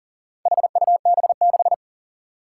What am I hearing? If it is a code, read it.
Morse code, heard as a keyed tone, about 38 wpm, 700 Hz. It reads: HUB6
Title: Four Challenging Letter-Number Combinations 40wpm